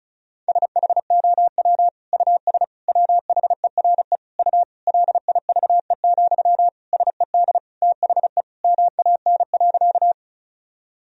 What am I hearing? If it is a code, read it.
Morse code, heard as a keyed tone, about 36 wpm, 700 Hz. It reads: SHOW US WHERE U LIVE, SED THE MAN.
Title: Sentences from Top 200 Words and Abbreviations 35wpm